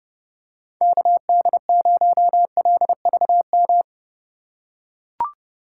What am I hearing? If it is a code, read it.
Morse code, heard as a keyed tone, about 32 wpm, 700 Hz. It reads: KD0LVM
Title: USA Call Signs - 6 Characters - 2x Character Spacing - 30wpm